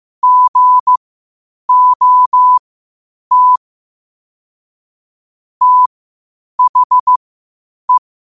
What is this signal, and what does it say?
Morse code, heard as a keyed tone, about 15 wpm, spaced as for 8 wpm, 1000 Hz. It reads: GOT THE
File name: qotdfiles / 8wpm